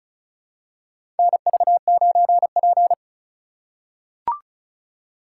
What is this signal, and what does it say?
Morse code, heard as a keyed tone, about 36 wpm, 700 Hz. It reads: NV9P
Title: USA Call Signs - 3x Character Spacing - 35wpm